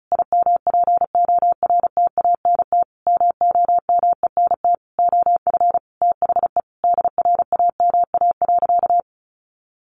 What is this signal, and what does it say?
Morse code, heard as a keyed tone, about 34 wpm, 700 Hz. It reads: IMPORTANT MOMENT OF THE DRAMA.
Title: Sentences from Top 1500 Words 35wpm